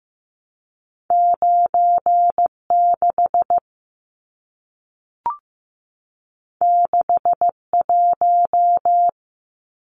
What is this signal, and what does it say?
Morse code, heard as a keyed tone, about 15 wpm, 700 Hz. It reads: 96 61
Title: Numbers - 2-Digit 15wpm